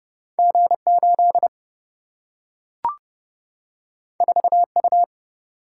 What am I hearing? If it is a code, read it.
Morse code, heard as a keyed tone, about 30 wpm, 700 Hz. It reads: G8 4U
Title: Single-Letter Single-Number 30wpm